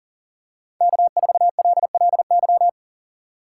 Morse code, heard as a keyed tone, about 40 wpm, 700 Hz. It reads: K4LLY
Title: USA Call Signs - 5 Characters - 3x Character Spacing - 40wpm